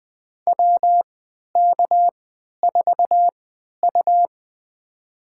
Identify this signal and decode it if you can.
Morse code, heard as a keyed tone, about 20 wpm, spaced as for 11 wpm, 700 Hz. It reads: WK4U
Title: USA Call Signs - 4 Characters - 3x Character Spacing - 20wpm